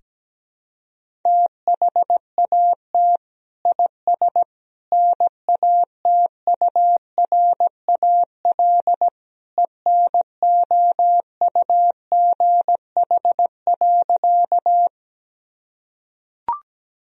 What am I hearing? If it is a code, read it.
Morse code, heard as a keyed tone, about 17 wpm, 700 Hz. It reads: THAT IS NATURAL ENOUGH.
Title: Sentences from Top 1000 Words - Review Part 4 - 17wpm